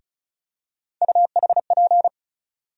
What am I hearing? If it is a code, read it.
Morse code, heard as a keyed tone, about 36 wpm, 700 Hz. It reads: UHP
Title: USA Call Sign Postfixes - 3 Letters 35wpm